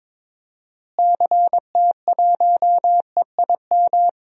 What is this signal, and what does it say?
Morse code, heard as a keyed tone, about 22 wpm, 700 Hz. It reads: CT1EIM
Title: DXCC-100 Call Signs - 6 Characters - Repeated 3x - 22wpm